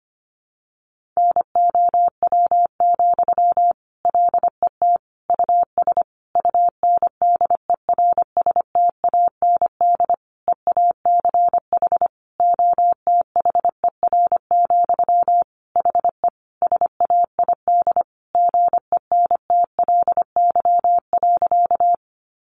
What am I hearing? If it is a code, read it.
Morse code, heard as a keyed tone, about 25 wpm, 700 Hz. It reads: NOW, LET US UNDERSTAND EACH OTHER, HE SAID GENTLY.
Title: Sentences from Top 1500 Words 25wpm